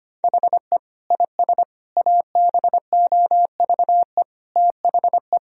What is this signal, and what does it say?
Morse code, heard as a keyed tone, about 25 wpm, 700 Hz. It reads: HE IS ABOVE THE
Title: Sentences from Top 1000 Words - Review Part 3 - 25wpm